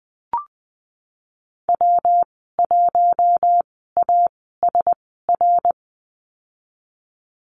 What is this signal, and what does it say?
Morse code, heard as a keyed tone, about 19 wpm, spaced as for 14 wpm, 700 Hz. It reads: W1ASR
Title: 154 Lesson 25g - Letter p - Call Signs 20wpm